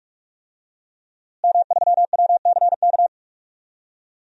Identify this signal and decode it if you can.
Morse code, heard as a keyed tone, about 46 wpm, 700 Hz. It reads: M3WCK
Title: DXCC-100 Call Signs - 5 Characters - Repeated 2x - 45wpm